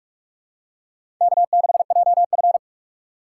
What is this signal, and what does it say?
Morse code, heard as a keyed tone, about 44 wpm, 700 Hz. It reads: K6JF
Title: USA Call Signs - 4 Characters - 8x Character Spacing - 45wpm